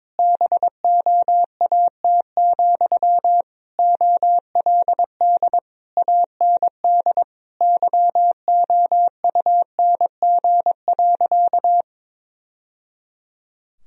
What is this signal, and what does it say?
Morse code, heard as a keyed tone, about 21 wpm, 700 Hz. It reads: BOAT, OLD AND YOUNG.
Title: Sentences from Top 1000 Words - Review Part 3 - 22wpm